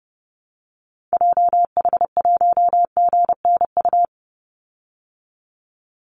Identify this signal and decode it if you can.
Morse code, heard as a keyed tone, about 30 wpm, 700 Hz. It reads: JH1GNU
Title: Call Signs Repeated 2x - 30wpm